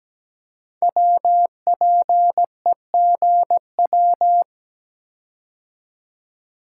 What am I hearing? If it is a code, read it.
Morse code, heard as a keyed tone, about 17 wpm, 700 Hz. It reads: WPEGW
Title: Code Groups - 5 Letter:Numbers - 17wpm